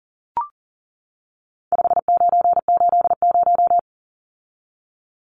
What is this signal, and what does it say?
Morse code, heard as a keyed tone, about 40 wpm, 700 Hz. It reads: 5980
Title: Numbers - 4-Digit 40wpm